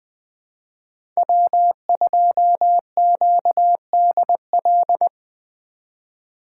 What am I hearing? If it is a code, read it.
Morse code, heard as a keyed tone, about 21 wpm, 700 Hz. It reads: W2QDL
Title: USA Call Signs - 4x Character Spacing - 20wpm